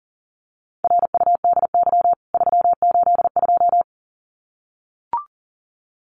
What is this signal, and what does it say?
Morse code, heard as a keyed tone, about 40 wpm, 700 Hz. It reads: RUDY 382
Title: CWT Exchange 40wpm